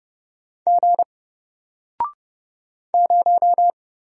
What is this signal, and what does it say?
Morse code, heard as a keyed tone, about 30 wpm, 700 Hz. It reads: G 0
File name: Single-Letter-Number-30wpm